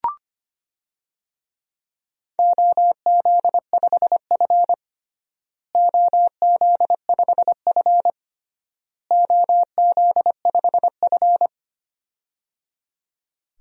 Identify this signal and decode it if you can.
Morse code, heard as a keyed tone, about 25 wpm, 700 Hz. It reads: OZ5F OZ5F OZ5F
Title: DXCC-100 Call Signs - 4 Characters - Repeated 3x - 25wpm